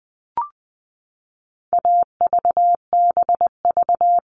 Morse code, heard as a keyed tone, about 20 wpm, 700 Hz. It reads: AVBV
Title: All Cut Numbers - 4-Digit 20wpm